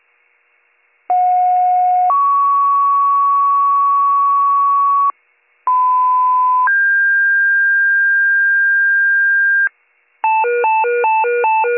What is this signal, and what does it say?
Two-tone page, heard as a one-second tone, then a three-second tone, about 730 then 1120 Hz, about 1010 then 1600 Hz. It is heard from 1.1 to 9.7 s.